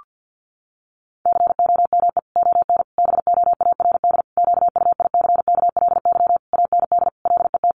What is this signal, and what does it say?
Morse code, heard as a keyed tone, about 50 wpm, 700 Hz. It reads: COME ON BOARD QUICKLY AND LET
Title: Sentences from Top 1300 Words 50wpm